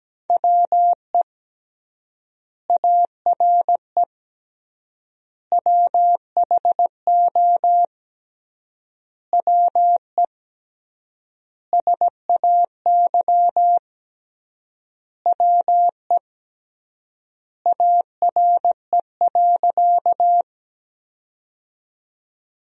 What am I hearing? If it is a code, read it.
Morse code, heard as a keyed tone, about 17 wpm, 700 Hz. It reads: WE ARE WHO WE SAY WE ARE.
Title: 3x Word-Spacing - Sentences from Top 200 Words - 17wpm